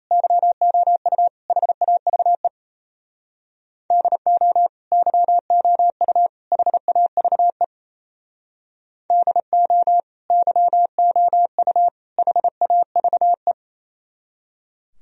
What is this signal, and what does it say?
Morse code, heard as a keyed tone, about 36 wpm, 700 Hz. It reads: YOU HAVE DO YOU HAVE DO YOU HAVE